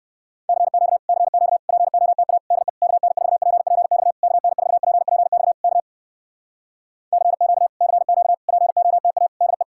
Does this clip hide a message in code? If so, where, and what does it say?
Yes: Morse code from 0.5 to 9.6 s, heard as a keyed tone, about 59 wpm, 700 Hz. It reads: CQ CQ POTA DE KN4PPX KN4PPX K CQ CQ POTA DE